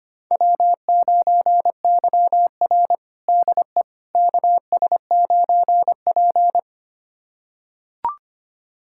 Morse code, heard as a keyed tone, about 25 wpm, 700 Hz. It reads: W9YR DE KS9P
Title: USA Callsign DE Callsign - 4 Characters - 25wpm